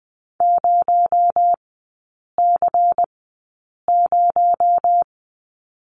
Morse code, heard as a keyed tone, about 20 wpm, spaced as for 8 wpm, 700 Hz. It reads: 0C0
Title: 218 Lesson 35e - Number 0 - Three Characters 20wpm